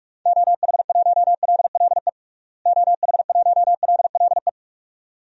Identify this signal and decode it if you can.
Morse code, heard as a keyed tone, about 46 wpm, 700 Hz. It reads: OH1LLE OH1LLE
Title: DXCC-100 Call Signs - 6 Characters - Repeated 2x - 45wpm